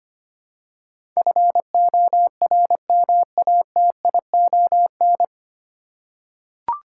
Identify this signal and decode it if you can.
Morse code, heard as a keyed tone, about 25 wpm, 700 Hz. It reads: FORMATION
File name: practice / 25wpm / 9 Letter Words - First 3 Letters Given - 25wpm